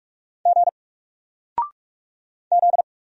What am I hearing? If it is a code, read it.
Morse code, heard as a keyed tone, about 44 wpm, 700 Hz. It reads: G Z
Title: Single Letter-Number 45wpm